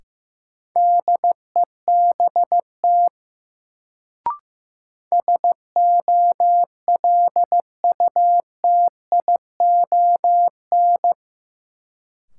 Morse code, heard as a keyed tone, about 15 wpm, 700 Hz. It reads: DEBT SOLUTION
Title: Sentences from Top 1300 Words - New Word Drill Alt - 15wpm